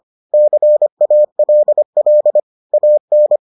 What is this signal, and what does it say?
Morse code, heard as a keyed tone, about 25 wpm, 600 Hz. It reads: CALL AN